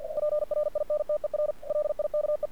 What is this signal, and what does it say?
Morse code, heard as a keyed tone, about 50 wpm, 600 Hz. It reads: PRINTER LIKE
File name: archived / CRSnet